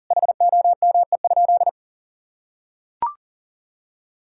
Morse code, heard as a keyed tone, about 40 wpm, 700 Hz. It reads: HOME?
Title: Sentences from Top 1600 Words 40wpm